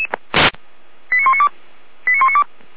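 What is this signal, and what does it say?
Five-tone selective call (ZVEI), heard as quick runs of five tones, tones of about 80 ms. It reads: 89192 89192
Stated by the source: ZVEI five-tone selective call received by SDR, fire brigade Orlamünde (feuerwehr-orlamuende).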